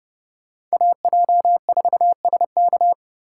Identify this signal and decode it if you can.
Morse code, heard as a keyed tone, about 38 wpm, 700 Hz. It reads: AJ4SK